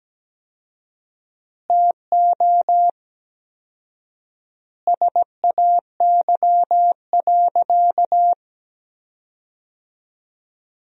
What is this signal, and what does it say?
Morse code, heard as a keyed tone, about 17 wpm, 700 Hz. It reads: TO SAY.
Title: 4x Word-Spacing - Sentences from Top 500 Words - 17wpm